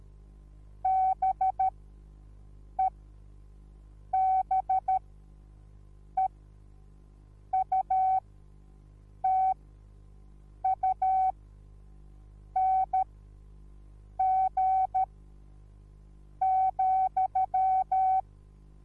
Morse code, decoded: BEBEUTUNG,